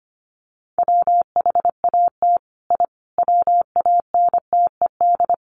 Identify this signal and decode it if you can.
Morse code, heard as a keyed tone, about 25 wpm, 700 Hz. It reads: WHAT I WANTED
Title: Sets of 5 Words Encore2 25wpm